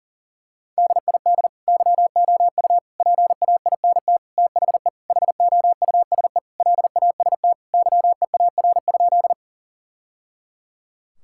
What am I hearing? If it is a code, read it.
Morse code, heard as a keyed tone, about 40 wpm, 700 Hz. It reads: DID YOU PAINT THE HOUSE LAST YEAR?
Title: Sentences from Top 1000 Words - Review Part 6 - 40wpm